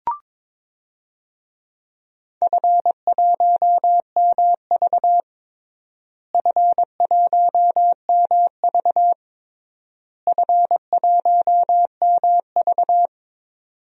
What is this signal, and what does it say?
Morse code, heard as a keyed tone, about 22 wpm, 700 Hz. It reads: F1MV F1MV F1MV
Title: DXCC-100 Call Signs - 4 Characters - Repeated 3x - 22wpm